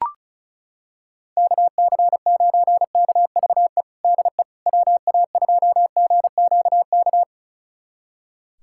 Morse code, heard as a keyed tone, about 34 wpm, 700 Hz. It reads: KC9KVE DE WA2GQK
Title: USA Callsign DE Callsign - 6 Characters - 35wpm